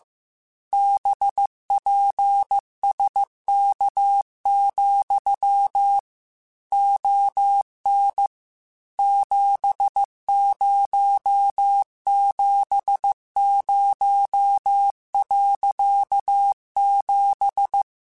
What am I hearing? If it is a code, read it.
Morse code, heard as a keyed tone, about 15 wpm, 800 Hz. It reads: BPSK, ON 7070.7
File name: QSOs / QSOs-mp3s / WA7QOX 2@13